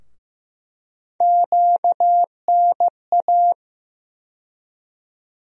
Morse code, heard as a keyed tone, about 15 wpm, 700 Hz. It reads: QNA
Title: USA Call Sign Postfixes - 3 Letters 15wpm